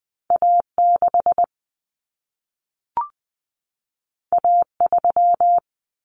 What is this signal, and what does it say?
Morse code, heard as a keyed tone, about 20 wpm, 700 Hz. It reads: A6 A3
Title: Common Cut Numbers - 2-Digit 20wpm